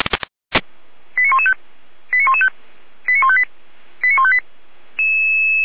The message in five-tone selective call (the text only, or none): 89115 89115 89158 89158